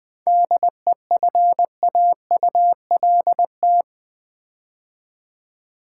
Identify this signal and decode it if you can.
Morse code, heard as a keyed tone, about 20 wpm, 700 Hz. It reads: DEFAULT
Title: 7 Letter Words - First 3 Letters Given - 20wpm